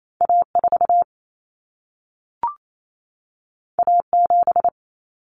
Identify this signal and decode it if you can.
Morse code, heard as a keyed tone, about 28 wpm, 700 Hz. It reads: A4 A7
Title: Common Cut Numbers - 2-Digit 28wpm